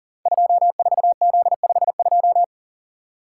Morse code, heard as a keyed tone, about 40 wpm, 690 Hz. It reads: 24Z52
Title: Code Groups - 5 Letter:Numbers - 40wpm